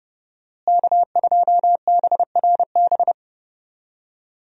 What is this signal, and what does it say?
Morse code, heard as a keyed tone, about 32 wpm, 700 Hz. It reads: K2BRB
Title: USA Call Signs - 5 Characters - Repeated 2x - 30wpm